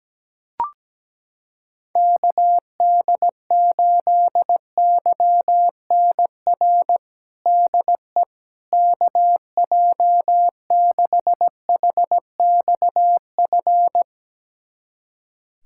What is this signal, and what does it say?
Morse code, heard as a keyed tone, about 17 wpm, 700 Hz. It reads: KD8YNR DE KJ6HXF
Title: USA Callsign DE Callsign 17wpm